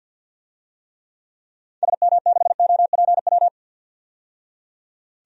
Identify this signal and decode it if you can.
Morse code, heard as a keyed tone, about 49 wpm, 700 Hz. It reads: SM6OPW
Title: DXCC-100 Call Signs - 6 Characters - 50wpm